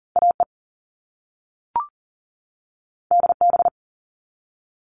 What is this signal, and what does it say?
Morse code, heard as a keyed tone, about 40 wpm, 700 Hz. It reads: AE DB
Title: All Cut Numbers - 2-Digit 40wpm